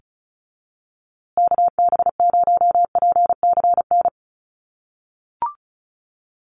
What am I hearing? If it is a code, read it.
Morse code, heard as a keyed tone, about 34 wpm, 700 Hz. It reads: KB0PCN
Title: USA Call Signs Repeated 2x 35wpm